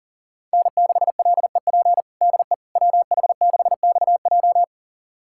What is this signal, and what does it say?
Morse code, heard as a keyed tone, about 40 wpm, 700 Hz. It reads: N6LEP DE WH6XJ
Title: USA Callsign DE Callsign - 5 Characters - 40wpm